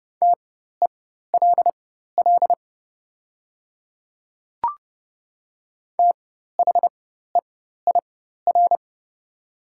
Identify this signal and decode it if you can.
Morse code, heard as a keyed tone, about 30 wpm, spaced as for 14 wpm, 700 Hz. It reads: TELL THEIR
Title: 4x Character Spacing - Top 200 Words - 30wpm